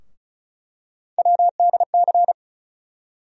Morse code, heard as a keyed tone, about 36 wpm, 700 Hz. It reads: WDC
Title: USA Call Sign Postfixes - 3 Letters 35wpm